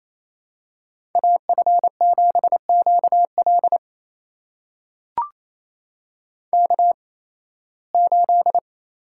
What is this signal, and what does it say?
Morse code, heard as a keyed tone, about 28 wpm, spaced as for 18 wpm, 700 Hz. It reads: AF7QL K8
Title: USA Call Signs - 5 Characters - 8x Character Spacing - 28wpm